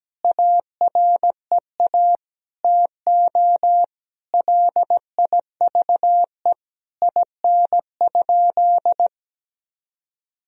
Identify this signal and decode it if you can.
Morse code, heard as a keyed tone, about 17 wpm, 700 Hz. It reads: AREA TO LIVE IN?
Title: Sentences from Top 1000 Words - Review Part 2 - 17wpm